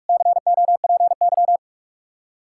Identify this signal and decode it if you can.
Morse code, heard as a keyed tone, about 55 wpm, 700 Hz. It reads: COPY
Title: Speed-Racing Common QSO Phrases Encore - 45wpm